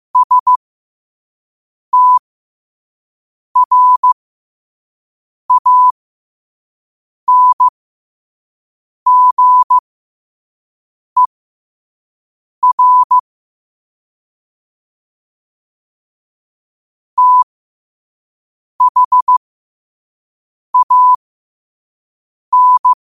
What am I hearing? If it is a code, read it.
Morse code, heard as a keyed tone, about 15 wpm, spaced as for 5 wpm, 1000 Hz. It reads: STRANGER THAN
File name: 5wpm